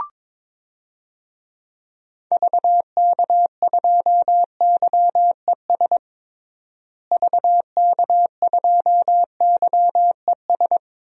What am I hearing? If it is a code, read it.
Morse code, heard as a keyed tone, about 22 wpm, 700 Hz. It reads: VK2YES VK2YES
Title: DXCC-100 Call Signs - 6 Characters - Repeated 2x - 22wpm